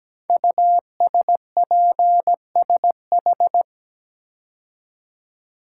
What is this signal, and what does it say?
Morse code, heard as a keyed tone, about 17 wpm, 700 Hz. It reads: USPSH
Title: Code Groups - 5 Letters - 17wpm